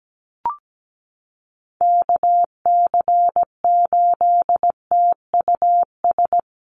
Morse code, heard as a keyed tone, about 17 wpm, 700 Hz. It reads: KC8TUS
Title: USA Call Signs Repeated 2x 17wpm